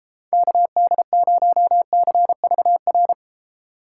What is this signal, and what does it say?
Morse code, heard as a keyed tone, about 32 wpm, 700 Hz. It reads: KD0CVR